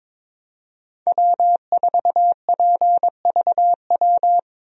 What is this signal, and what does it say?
Morse code, heard as a keyed tone, about 22 wpm, 700 Hz. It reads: W4PVW
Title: USA Call Signs - 5 Characters - Repeated 3x - 22wpm